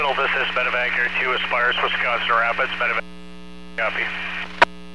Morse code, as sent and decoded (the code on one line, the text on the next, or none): none